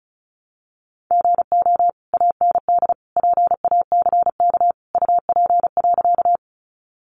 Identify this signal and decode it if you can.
Morse code, heard as a keyed tone, about 34 wpm, 700 Hz. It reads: GO AND PACK UP.
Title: Sentences from Top 1300 Words 35wpm